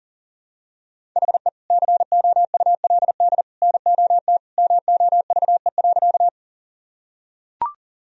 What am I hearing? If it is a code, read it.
Morse code, heard as a keyed tone, about 40 wpm, 700 Hz. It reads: HE COULD NOT MOVE.
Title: Sentences from Top 1000 Words - Review Part 1 - 40wpm